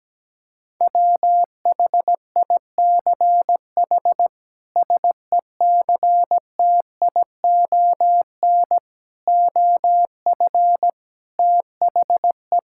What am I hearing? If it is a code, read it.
Morse code, heard as a keyed tone, about 17 wpm, 700 Hz. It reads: WHICH SECTION OF THE
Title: Sentences from Top 1000 Words - Review Part 5 - 17wpm